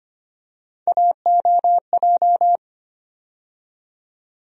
Morse code, heard as a keyed tone, about 25 wpm, 700 Hz. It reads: AOJ
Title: USA Call Sign Postfixes - 3 Letters 25wpm